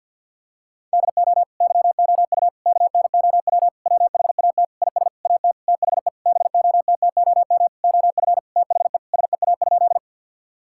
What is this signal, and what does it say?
Morse code, heard as a keyed tone, about 52 wpm, 700 Hz. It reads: DO YOU KNOW WHAT IS AT THE BOTTOM OF THE SEA?
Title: Sentences from Top 1000 Words - Review Part 6 - 50wpm